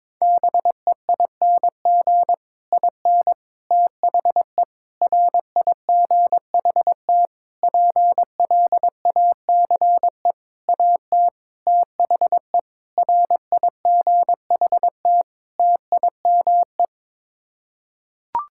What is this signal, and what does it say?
Morse code, heard as a keyed tone, about 22 wpm, 700 Hz. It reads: BEING IN THE RIGHT PLACE AT THE RIGHT TIME
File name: Sets of 9 Words - 22wpm